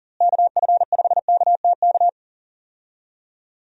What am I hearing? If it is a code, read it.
Morse code, heard as a keyed tone, about 38 wpm, 700 Hz. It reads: KF5KTK
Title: USA Call Signs - 6 Characters - Repeated 3x - 40wpm